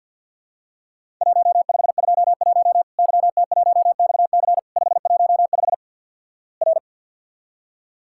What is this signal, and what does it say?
Morse code, heard as a keyed tone, about 50 wpm, 700 Hz. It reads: 1521 YT1XC 515 R
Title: Worked All Europe DX Contest - QTC Exchange - 50wpm